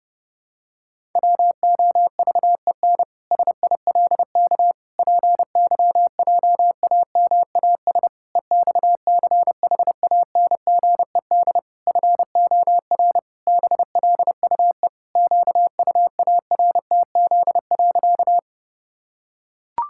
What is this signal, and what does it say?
Morse code, heard as a keyed tone, about 30 wpm, 700 Hz. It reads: WOVEN SILK PYJAMAS EXCHANGED FOR BLUE QUARTZ.
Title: Pangram Challenge v2 - 30wpm